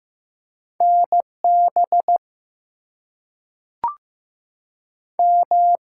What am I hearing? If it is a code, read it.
Morse code, heard as a keyed tone, about 15 wpm, 700 Hz. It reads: NB M
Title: US States and Canadian Provinces 15wpm